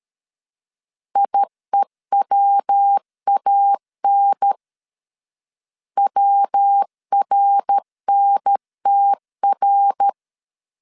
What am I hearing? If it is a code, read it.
Morse code, heard as a keyed tone, about 13 wpm, 800 Hz. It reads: IEWAN WRNTR
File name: CW / DET / 1 3 1500C S B 12wpm 2828